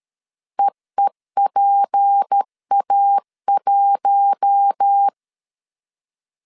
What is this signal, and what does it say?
Morse code, heard as a keyed tone, about 13 wpm, 800 Hz. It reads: EEPA1
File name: CW / DET / 1 6 1500C S A 12wpm 3440